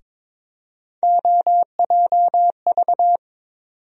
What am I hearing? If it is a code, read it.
Morse code, heard as a keyed tone, about 21 wpm, 700 Hz. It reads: OJV
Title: USA Call Sign Postfixes - 3 Letters 22wpm